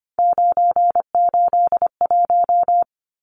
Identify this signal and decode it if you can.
Morse code, heard as a keyed tone, about 25 wpm, 700 Hz. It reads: 981
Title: Numbers - 3-Digit 25wpm